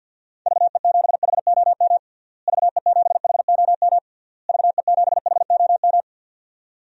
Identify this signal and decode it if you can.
Morse code, heard as a keyed tone, about 46 wpm, 700 Hz. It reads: VE7HOM VE7HOM VE7HOM
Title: DXCC-100 Call Signs - 6 Characters - Repeated 3x - 50wpm